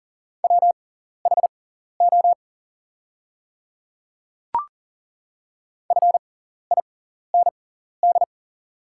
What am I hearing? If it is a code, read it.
Morse code, heard as a keyed tone, about 40 wpm, spaced as for 14 wpm, 690 Hz. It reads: WHO FIND
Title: 6x Character Spacing - Top 300 Words - 40wpm